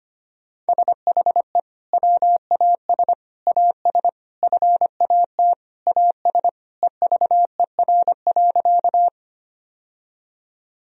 Morse code, heard as a keyed tone, about 25 wpm, 700 Hz. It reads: SHE WAS AS FAT AS EVER.
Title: Sentences from Top 1000 Words - Review Part 4 - 25wpm